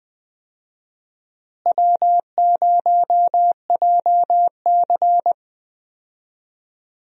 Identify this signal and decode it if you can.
Morse code, heard as a keyed tone, about 20 wpm, 700 Hz. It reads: W0JC
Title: USA Call Signs - 4 Characters - Repeated 3x - 20wpm